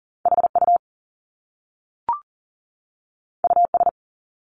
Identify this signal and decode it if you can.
Morse code, heard as a keyed tone, about 42 wpm, 700 Hz. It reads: HU US
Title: Top Two Letter Combinations in Words 40wpm